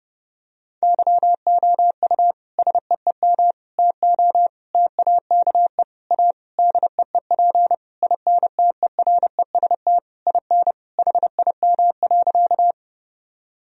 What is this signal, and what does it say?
Morse code, heard as a keyed tone, about 32 wpm, 700 Hz. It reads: YOU SEEM TO TAKE A DEEP INTEREST IN HIM.
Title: Sentences from Top 1000 Words - Review Part 3 - 30wpm